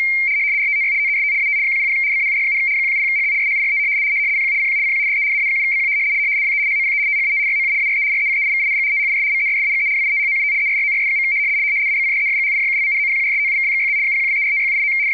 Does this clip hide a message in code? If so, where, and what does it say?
Yes: RTTY from 0.6 to 14.9 s, about 100 baud. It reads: RTTY TEST AT 100 BAUD, 170HZ SHIFT CQ CQ CQ DE AA7BQ WE HOPE THAT YOU ARE RECEIVING THIS MESSAGE WITHOUT ERRORS. ABCDEFGHIJKLMNOPQRSTUVWXYZ 0123456789